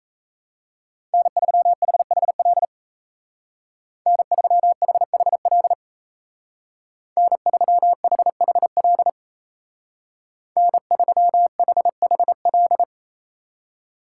Morse code, heard as a keyed tone, about 34 wpm, 700 Hz. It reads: N3HHL N3HHL N3HHL N3HHL
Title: Speed Racing - USA 1x3 Call Signs 28wpm